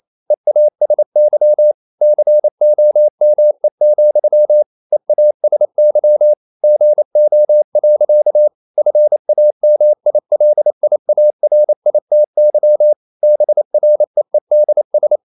Morse code, heard as a keyed tone, about 28 wpm, 600 Hz. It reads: EASY COME, EASY GO. FAMILIARITY BREEDS
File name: CW / Practice / EnglishSayings28